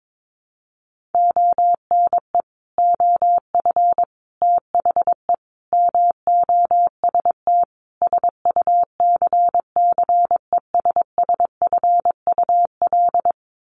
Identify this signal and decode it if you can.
Morse code, heard as a keyed tone, about 22 wpm, 700 Hz. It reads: ONE OF THE MOST SUCCESSFUL
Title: Sets of 5 Words Encore2 22wpm